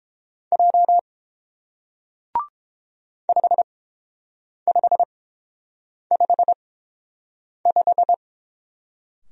Morse code, heard as a keyed tone, about 31 wpm, spaced as for 7 wpm, 700 Hz. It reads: J 5555